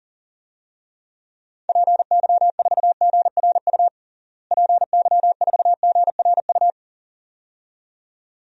Morse code, heard as a keyed tone, about 40 wpm, 700 Hz. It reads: PY4GRU PY4GRU
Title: DXCC-100 Call Signs - 6 Characters - Repeated 2x - 40wpm